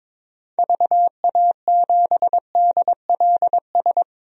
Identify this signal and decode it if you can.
Morse code, heard as a keyed tone, about 22 wpm, 700 Hz. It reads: VA7DLS
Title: DXCC-100 Call Signs - 6 Characters - 22wpm